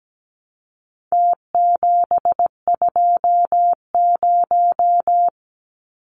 Morse code, heard as a keyed tone, about 17 wpm, 700 Hz. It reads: T720
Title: Numbers - Contest Style - 4-Digit 17wpm